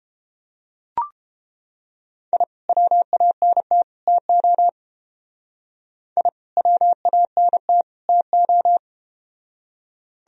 Morse code, heard as a keyed tone, about 30 wpm, 700 Hz. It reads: I WANT TO I WANT TO